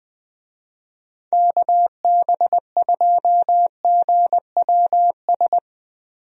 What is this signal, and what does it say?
Morse code, heard as a keyed tone, about 20 wpm, 700 Hz. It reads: KB2GWS
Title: USA Call Signs - 6 Characters - Repeated 2x - 20wpm